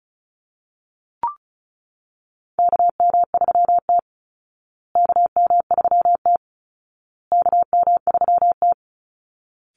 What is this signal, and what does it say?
Morse code, heard as a keyed tone, about 34 wpm, 700 Hz. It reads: KM3T KM3T KM3T
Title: Call Signs Repeated 2x - 35wpm